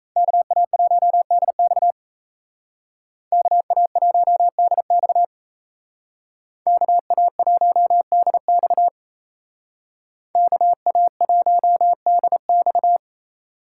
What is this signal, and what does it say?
Morse code, heard as a keyed tone, about 36 wpm, 700 Hz. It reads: KA1DX KA1DX KA1DX KA1DX